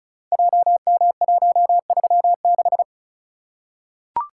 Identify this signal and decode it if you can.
Morse code, heard as a keyed tone, about 36 wpm, 700 Hz. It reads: JM136
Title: Code Groups - 5 Letter:Numbers - 35wpm